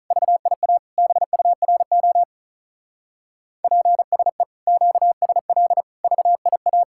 Morse code, heard as a keyed tone, about 46 wpm, 700 Hz. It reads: VIA BURO PSE QSL VIA